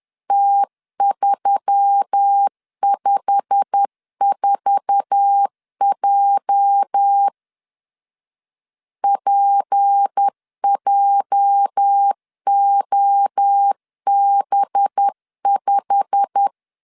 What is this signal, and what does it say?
Morse code, heard as a keyed tone, about 10 wpm, 800 Hz. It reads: T354J PJOB5
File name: LEZIONI / 8 YJ45 / 8 YJ45 10wpm 2549